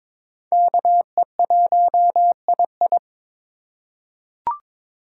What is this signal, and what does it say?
Morse code, heard as a keyed tone, about 22 wpm, 700 Hz. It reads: KE1II
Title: USA Call Signs - 4x Character Spacing - 22wpm